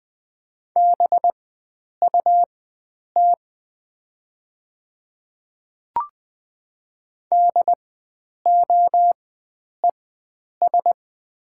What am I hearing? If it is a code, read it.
Morse code, heard as a keyed tone, about 20 wpm, spaced as for 9 wpm, 700 Hz. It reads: BUT DOES
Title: 4x Character Spacing - Top 300 Words - 20wpm